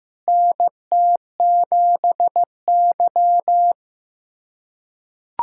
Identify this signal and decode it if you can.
Morse code, heard as a keyed tone, about 15 wpm, 700 Hz. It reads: NT7Y
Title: USA Callsign DE Callsign - 4 Characters - 15wpm